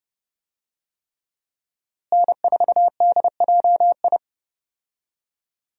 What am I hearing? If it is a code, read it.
Morse code, heard as a keyed tone, about 32 wpm, 700 Hz. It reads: N4DJI
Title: USA Call Signs - 5 Characters - Repeated 3x - 30wpm